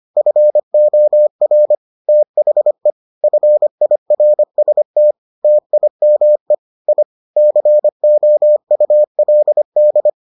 Morse code, heard as a keyed tone, about 25 wpm, 600 Hz. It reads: FOR THE FIRST TIME I COULD